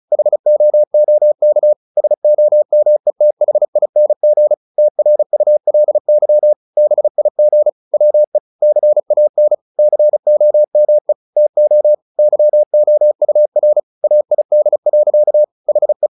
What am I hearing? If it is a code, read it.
Morse code, heard as a keyed tone, about 34 wpm, 600 Hz. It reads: HOOK SOMETHING TRULY BIG WE CAN COME TO YOUR AID. HE
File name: the old man and the sea / the old man and the sea 1 35wpm